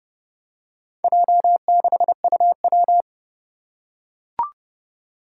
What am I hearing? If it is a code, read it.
Morse code, heard as a keyed tone, about 30 wpm, 700 Hz. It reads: J6UW
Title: Four Challenging Letter-Number Combinations 30wpm